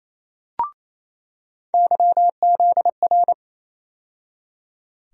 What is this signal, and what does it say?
Morse code, heard as a keyed tone, about 28 wpm, 700 Hz. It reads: YZR
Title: USA Call Sign Postfixes - 3 Letters 28wpm